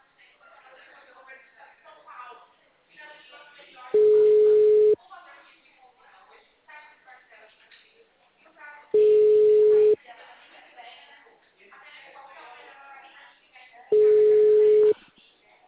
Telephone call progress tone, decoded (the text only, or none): ringback tone